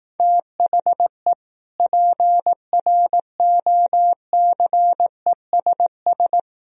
Morse code, heard as a keyed tone, about 18 wpm, 700 Hz. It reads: THE PROCESS
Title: Speed-Racing - Sets of 3 Words 15wpm